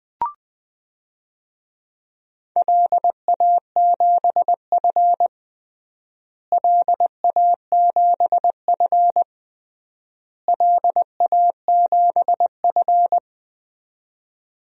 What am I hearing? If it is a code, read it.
Morse code, heard as a keyed tone, about 20 wpm, 700 Hz. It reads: LA7F LA7F LA7F
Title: DXCC-100 Call Signs - 4 Characters - Repeated 3x - 20wpm